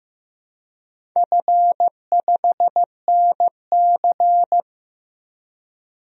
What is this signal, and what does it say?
Morse code, heard as a keyed tone, about 15 wpm, 700 Hz. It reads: F5NC
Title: DXCC-100 Call Signs - 4 Characters - 15wpm